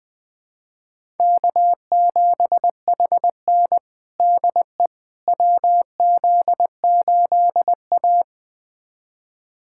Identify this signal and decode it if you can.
Morse code, heard as a keyed tone, about 20 wpm, 700 Hz. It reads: K7HN DE WZ8A
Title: USA Callsign DE Callsign - 4 Characters - 20wpm